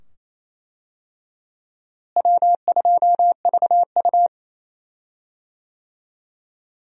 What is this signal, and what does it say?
Morse code, heard as a keyed tone, about 28 wpm, 700 Hz. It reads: W2VU
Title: USA Call Signs - 4 Characters - Repeated 2x - 28wpm